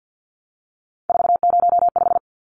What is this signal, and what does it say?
Morse code, heard as a keyed tone, about 50 wpm, 700 Hz. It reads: 405
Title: Numbers - 3-Digit 50wpm